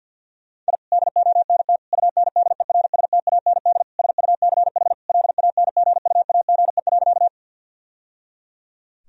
Morse code, heard as a keyed tone, about 52 wpm, 700 Hz. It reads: I DONT UNDERSTAND SUCH LANGUAGE.
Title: Sentences from Top 1000 Words - Review Part 3 - 50wpm